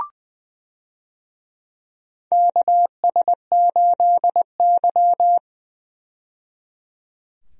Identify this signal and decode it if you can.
Morse code, heard as a keyed tone, about 20 wpm, 700 Hz. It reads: KS8Y
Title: USA Call Signs - 4 Characters - 20wpm